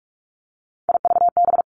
Morse code, heard as a keyed tone, about 46 wpm, 700 Hz. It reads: IVB
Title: Three Challenging Letter-Number Combinations 45wpm